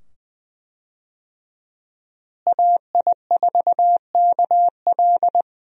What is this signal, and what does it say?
Morse code, heard as a keyed tone, about 20 wpm, 700 Hz. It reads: AI4KL
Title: USA Call Signs - 5 Characters - Repeated 2x - 20wpm